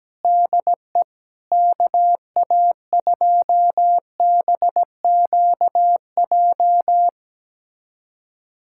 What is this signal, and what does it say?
Morse code, heard as a keyed tone, about 17 wpm, 700 Hz. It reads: DE KA2BQJ
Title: USA Callsign DE Callsign - 6 Characters - 17wpm